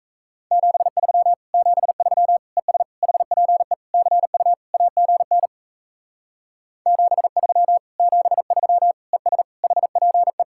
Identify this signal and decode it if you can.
Morse code, heard as a keyed tone, about 37 wpm, 700 Hz. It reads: 73 73 ES HPE CU AGN 73 73 ES HPE